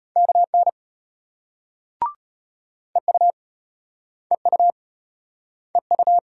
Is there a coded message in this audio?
Yes: Morse code from 0.2 to 6.2 s, heard as a keyed tone, about 35 wpm, 700 Hz. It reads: KN EU EU EU